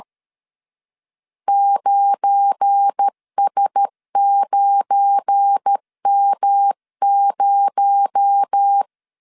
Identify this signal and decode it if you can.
Morse code, heard as a keyed tone, about 13 wpm, 800 Hz. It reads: 9S9M0
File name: LEZIONI / 9 286 12wpm 2305